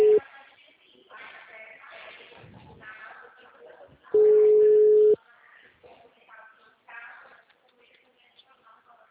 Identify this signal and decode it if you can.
Ringback tone (Europe, 425 Hz), with speech.